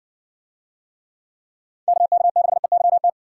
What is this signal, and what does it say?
Morse code, heard as a keyed tone, about 56 wpm, 700 Hz. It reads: KK6EYT